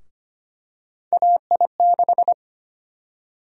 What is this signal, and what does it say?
Morse code, heard as a keyed tone, about 25 wpm, 700 Hz. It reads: AI6
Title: USA Call Sign Prefixes - Repeated 2x - 25wpm